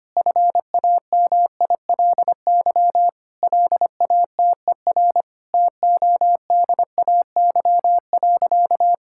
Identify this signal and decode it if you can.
Morse code, heard as a keyed tone, about 25 wpm, 700 Hz. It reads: FAMILY LATER TODAY.
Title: Sentences from Top 1600 Words 25wpm